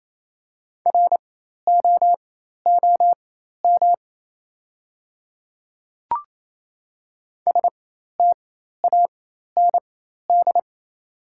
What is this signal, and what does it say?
Morse code, heard as a keyed tone, about 28 wpm, spaced as for 13 wpm, 700 Hz. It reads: ROOM STAND
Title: 4x Character Spacing - Top 500 Words - 28wpm